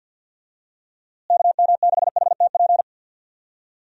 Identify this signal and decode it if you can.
Morse code, heard as a keyed tone, about 46 wpm, 700 Hz. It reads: KM6HTP
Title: USA Call Signs - 6 Characters - Repeated 2x - 50wpm